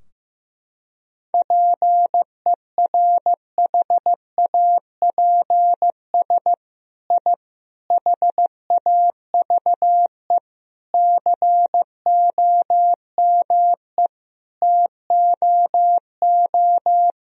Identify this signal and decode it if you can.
Morse code, heard as a keyed tone, about 15 wpm, 700 Hz. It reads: PERHAPS I HAVE COME TOO
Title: Sentences from Top 1000 Words - Review Part 2 - 15wpm